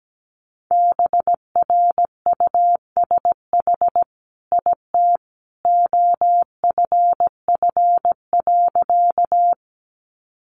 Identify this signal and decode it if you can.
Morse code, heard as a keyed tone, about 17 wpm, 700 Hz. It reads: BRUSH IT OFF.
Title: Sentences from Top 1100 Words 17wpm